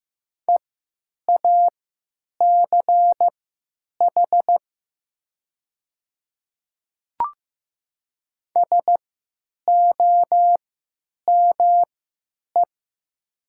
Morse code, heard as a keyed tone, about 15 wpm, spaced as for 9 wpm, 700 Hz. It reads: EACH SOME
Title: 3x Character Spacing - Top 100 Words - 15wpm